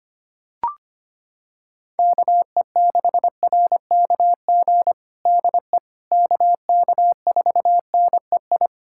Morse code, heard as a keyed tone, about 25 wpm, 700 Hz. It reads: KE6RKG DE KK4NEI
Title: USA Callsign DE Callsign 25wpm